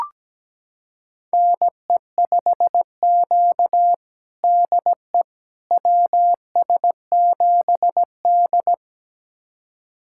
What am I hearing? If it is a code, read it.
Morse code, heard as a keyed tone, about 17 wpm, 700 Hz. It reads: NE5Q DE WS7D
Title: USA Callsign DE Callsign - 4 Characters - 17wpm